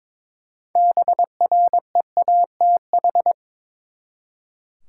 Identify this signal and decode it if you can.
Morse code, heard as a keyed tone, about 22 wpm, 700 Hz. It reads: BREATH
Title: Six Letter Words - 22wpm